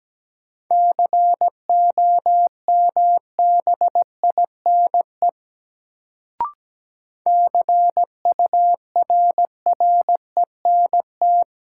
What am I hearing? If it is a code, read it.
Morse code, heard as a keyed tone, about 17 wpm, 700 Hz. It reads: COMBINE CURRENT
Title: Sentences from Top 1600 Words - New Word Drill Alt - 17wpm